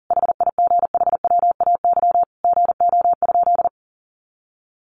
Morse code, heard as a keyed tone, about 40 wpm, 700 Hz. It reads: HIGHWAY GO?
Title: Sentences from Top 1400 Words 40wpm